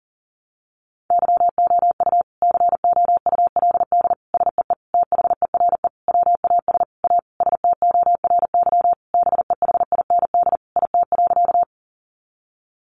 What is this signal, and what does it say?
Morse code, heard as a keyed tone, about 40 wpm, 700 Hz. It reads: YOU COULD SEE THERE WAS A STORY BEHIND IT.
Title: Sentences from Top 500 Words 40wpm